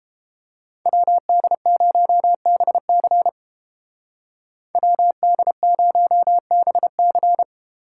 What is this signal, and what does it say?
Morse code, heard as a keyed tone, about 29 wpm, 700 Hz. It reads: WD0BC WD0BC